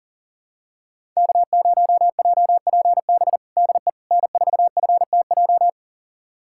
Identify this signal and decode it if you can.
Morse code, heard as a keyed tone, about 40 wpm, 700 Hz. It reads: K0JPB DE N4FTJ
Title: USA Callsign DE Callsign - 5 Characters - 40wpm